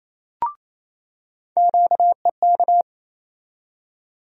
Morse code, heard as a keyed tone, about 28 wpm, 700 Hz. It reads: QEK
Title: USA Call Sign Postfixes - 3 Letters 28wpm